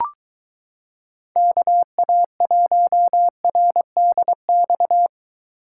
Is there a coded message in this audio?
Yes: Morse code from 1.4 to 5.1 s, heard as a keyed tone, about 22 wpm, 700 Hz. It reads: KA1RDX